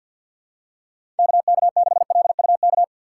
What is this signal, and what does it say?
Morse code, heard as a keyed tone, about 47 wpm, 700 Hz. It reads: KK6LUK
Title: USA Call Signs - 6 Characters - 4x Character Spacing - 50wpm